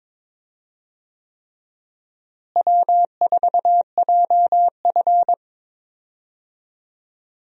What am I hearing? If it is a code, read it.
Morse code, heard as a keyed tone, about 22 wpm, 700 Hz. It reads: W4JF
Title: USA Call Signs - 4 Characters - Repeated 3x - 22wpm